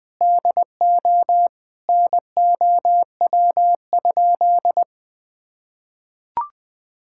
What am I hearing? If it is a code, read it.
Morse code, heard as a keyed tone, about 20 wpm, 700 Hz. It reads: DO NOW?
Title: Sentences with 6-Word Phrases - 20wpm